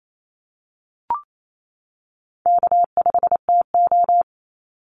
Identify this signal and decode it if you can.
Morse code, heard as a keyed tone, about 28 wpm, 700 Hz. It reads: K5TO
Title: Easy Call Signs - 28wpm